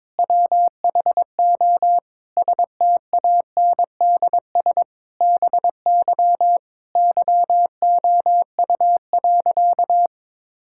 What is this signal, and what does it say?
Morse code, heard as a keyed tone, about 22 wpm, 700 Hz. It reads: WHO STANDS BY YOU.
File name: Difficult Sentences with Hints - 22wpm